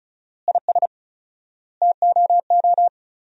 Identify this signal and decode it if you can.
Morse code, heard as a keyed tone, about 36 wpm, 700 Hz. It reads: IS TOO
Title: 4x Word-Spacing - Sentences from Top 400 Words - 35wpm